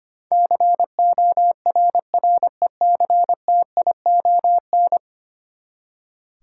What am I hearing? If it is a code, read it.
Morse code, heard as a keyed tone, about 25 wpm, 700 Hz. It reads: CORRECTION
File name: Ten Letter Words - 25wpm